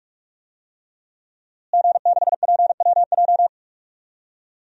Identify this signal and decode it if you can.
Morse code, heard as a keyed tone, about 46 wpm, 700 Hz. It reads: G6PWJ
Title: DXCC-100 Call Signs - 5 Characters - Repeated 3x - 45wpm